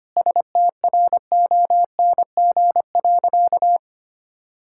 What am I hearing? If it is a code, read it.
Morse code, heard as a keyed tone, about 25 wpm, 700 Hz. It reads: STRONG.
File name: practice / 25wpm / Sentences with 3-Word Phrases - 25wpm